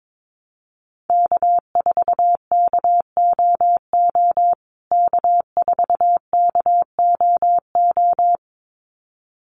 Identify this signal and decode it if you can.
Morse code, heard as a keyed tone, about 22 wpm, 700 Hz. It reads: K4KOO K4KOO
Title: USA Call Signs Repeated 2x 22wpm